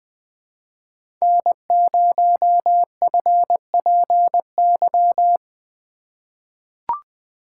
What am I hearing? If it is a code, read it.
Morse code, heard as a keyed tone, about 20 wpm, 700 Hz. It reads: N0FPY
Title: USA Call Signs - 5 Characters - 4x Character Spacing - 20wpm